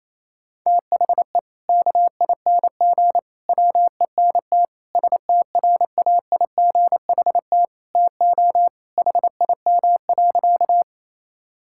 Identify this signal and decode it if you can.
Morse code, heard as a keyed tone, about 28 wpm, 700 Hz. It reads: THE KING WENT STRAIGHT TO HIM.
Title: Sentences from Top 1000 Words - Review Part 6 - 28wpm